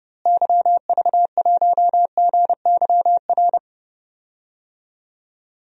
Morse code, heard as a keyed tone, about 30 wpm, 700 Hz. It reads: YV1GYR
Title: DXCC-100 Call Signs - 6 Characters - Repeated 2x - 30wpm